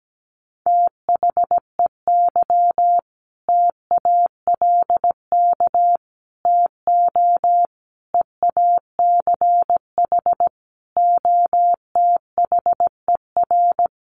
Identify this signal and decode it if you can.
Morse code, heard as a keyed tone, about 17 wpm, 700 Hz. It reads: THEY TALK TO EACH OTHER
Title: Sentences from Top 1400 Words 17wpm